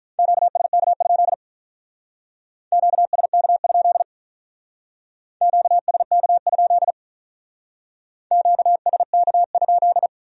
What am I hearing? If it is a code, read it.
Morse code, heard as a keyed tone, about 44 wpm, 700 Hz. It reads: QSK? QSK? QSK? QSK?